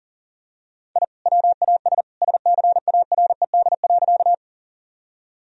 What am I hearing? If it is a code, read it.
Morse code, heard as a keyed tone, about 40 wpm, 700 Hz. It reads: I WAS SCARED.
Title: Sentences from Top 1000 Words - Review Part 4 - 40wpm